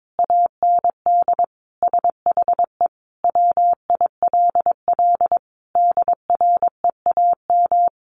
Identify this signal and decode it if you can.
Morse code, heard as a keyed tone, about 22 wpm, 700 Hz. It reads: AND SHE WILL DREAM
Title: Sentences from Top 1500 Words 22wpm